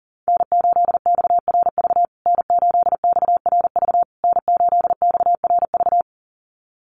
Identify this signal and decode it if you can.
Morse code, heard as a keyed tone, about 40 wpm, 700 Hz. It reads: N8XRV N8XRV N8XRV
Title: USA Call Signs Repeated 3x 40wpm